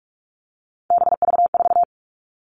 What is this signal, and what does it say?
Morse code, heard as a keyed tone, about 44 wpm, 700 Hz. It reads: BV4
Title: All Cut Numbers - 3-Digit 45wpm